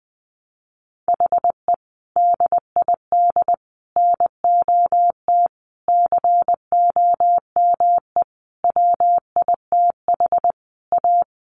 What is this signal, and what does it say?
Morse code, heard as a keyed tone, about 20 wpm, 700 Hz. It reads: HE DID NOT COME WITH A
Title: Sentences from Top 1200 Words 20wpm